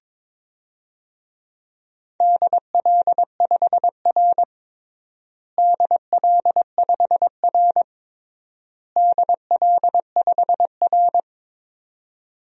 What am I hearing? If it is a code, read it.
Morse code, heard as a keyed tone, about 22 wpm, 700 Hz. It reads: DL5R DL5R DL5R
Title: DXCC-100 Call Signs - 4 Characters - Repeated 3x - 22wpm